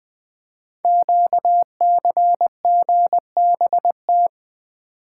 Morse code, heard as a keyed tone, about 20 wpm, 700 Hz. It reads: QCGBT
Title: Code Groups - 5 Letter:Numbers - 20wpm